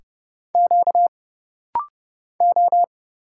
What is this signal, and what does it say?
Morse code, heard as a keyed tone, about 31 wpm, 700 Hz. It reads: Q O
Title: Single Letters - Rapid-Fire 30wpm